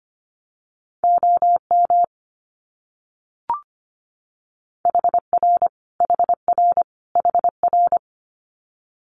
Morse code, heard as a keyed tone, about 25 wpm, 700 Hz. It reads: OM HR HR HR
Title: Abbreviations Repeated 3x - 25wpm